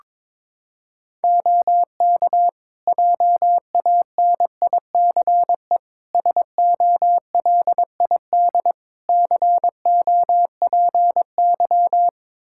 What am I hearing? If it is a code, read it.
Morse code, heard as a keyed tone, about 22 wpm, 700 Hz. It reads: OK JANICE SOLID COPY
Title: Practice QSO - Exchange 3 - More Info - 22wpm